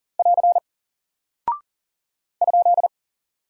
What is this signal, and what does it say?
Morse code, heard as a keyed tone, about 38 wpm, 700 Hz. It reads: <AR> ?
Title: Single Character (Letter:Number:Punctuation:Prosign) Alt 40wpm